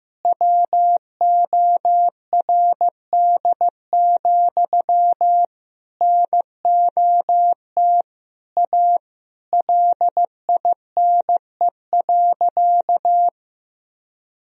Morse code, heard as a keyed tone, about 15 wpm, 700 Hz. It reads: WORD, NOT A LINE.
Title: Sentences from Top 1000 Words - Review Part 1 - 15wpm